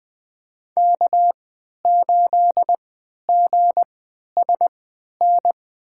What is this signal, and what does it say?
Morse code, heard as a keyed tone, about 20 wpm, spaced as for 11 wpm, 700 Hz. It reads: K8GSN
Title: USA Call Signs - 5 Characters - 3x Character Spacing - 20wpm